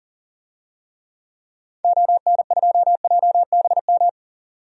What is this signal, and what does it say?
Morse code, heard as a keyed tone, about 37 wpm, 700 Hz. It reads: ON2JBM
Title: DXCC-100 Call Signs - 6 Characters - 40wpm